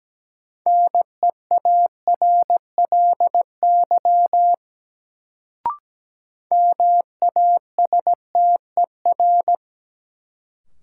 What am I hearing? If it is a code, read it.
Morse code, heard as a keyed tone, about 17 wpm, 700 Hz. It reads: NEARLY MASTER
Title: Sentences from Top 1300 Words - New Word Drill - 17wpm